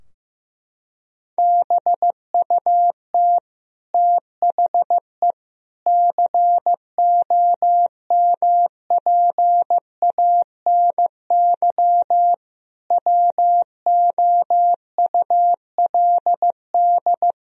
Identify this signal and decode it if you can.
Morse code, heard as a keyed tone, about 15 wpm, 700 Hz. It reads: BUT THE COMPANY WOULD
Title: Sentences from Top 1000 Words - Review Part 3 - 15wpm